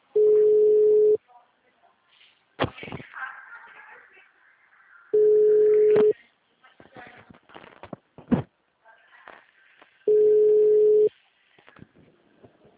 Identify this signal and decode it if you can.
Ringback tone (Europe, 425 Hz), with speech.
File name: out- -2018- - - .49